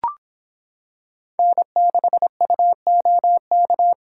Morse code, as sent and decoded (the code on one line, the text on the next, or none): -. -.... ..- --- -.-
N6UOK